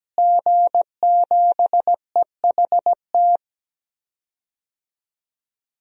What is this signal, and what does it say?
Morse code, heard as a keyed tone, about 17 wpm, 700 Hz. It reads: G7EHT
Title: DXCC-100 Call Signs - 5 Characters - Repeated 2x - 17wpm